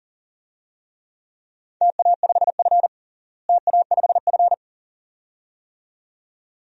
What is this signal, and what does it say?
Morse code, heard as a keyed tone, about 40 wpm, 700 Hz. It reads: TA5F TA5F
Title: DXCC-100 Call Signs - 4 Characters - Repeated 2x - 40wpm